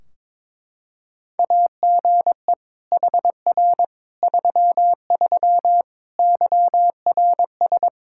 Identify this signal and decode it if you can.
Morse code, heard as a keyed tone, about 21 wpm, 700 Hz. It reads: AGE HR 33 YRS
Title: K1USN SST QSO Practice 22wpm